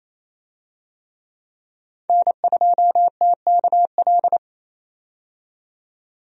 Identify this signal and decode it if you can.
Morse code, heard as a keyed tone, about 28 wpm, 700 Hz. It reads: N2TKL
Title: USA Call Signs - 5 Characters - 28wpm